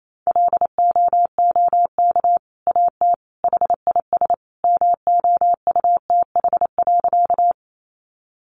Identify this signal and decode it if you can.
Morse code, heard as a keyed tone, about 28 wpm, 700 Hz. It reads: LOOK AT HIS MOUTH.
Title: Sentences from Top 1500 Words 28wpm